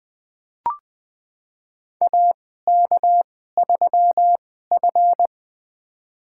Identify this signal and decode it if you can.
Morse code, heard as a keyed tone, about 20 wpm, spaced as for 14 wpm, 700 Hz. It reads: AK3F
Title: USA Call Signs - 4 Characters - 2x Character Spacing - 20wpm